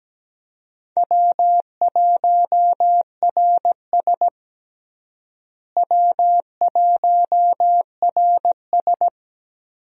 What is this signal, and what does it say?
Morse code, heard as a keyed tone, about 17 wpm, 700 Hz. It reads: W1RS W1RS
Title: USA Call Signs - 4 Characters - Repeated 3x - 17wpm